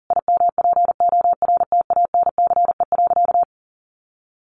Morse code, heard as a keyed tone, about 40 wpm, 700 Hz. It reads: IMPORTANCE.
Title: Sentences from Top 1200 Words 40wpm